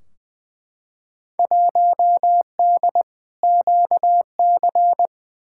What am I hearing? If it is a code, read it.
Morse code, heard as a keyed tone, about 21 wpm, 700 Hz. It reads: 1D QC
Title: Field Day Exchanges Encore - 20wpm